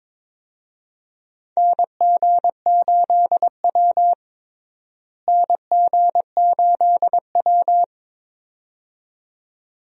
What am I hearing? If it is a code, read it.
Morse code, heard as a keyed tone, about 22 wpm, 700 Hz. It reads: NG8W NG8W
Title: USA Call Signs - 4 Characters - Repeated 2x - 22wpm